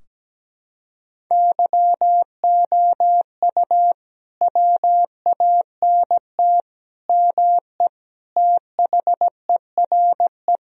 Morse code, heard as a keyed tone, about 17 wpm, 700 Hz. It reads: YOU WANT ME THERE
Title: Sentences from Top 1000 Words - Review Part 2 - 17wpm